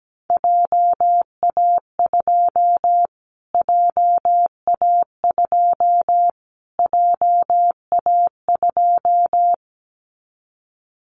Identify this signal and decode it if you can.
Morse code, heard as a keyed tone, about 17 wpm, 700 Hz. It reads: JA2 JA2 JA2
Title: Callsign Prefixes for Top 25 DXCC - Repeated 3x - 17wpm